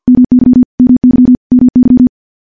Morse code, heard as a keyed tone, about 50 wpm, 250 Hz. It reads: MY MY MY
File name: morse / morse-training-50WPM